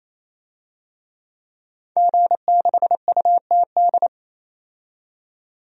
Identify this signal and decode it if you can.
Morse code, heard as a keyed tone, about 28 wpm, 700 Hz. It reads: G6UTD
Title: DXCC-100 Call Signs - 5 Characters - 28wpm